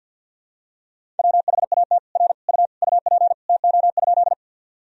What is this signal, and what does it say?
Morse code, heard as a keyed tone, about 52 wpm, 700 Hz. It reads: WHAT R U UP TO?
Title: Sentences from Top 100 Words and Abbreviations 50wpm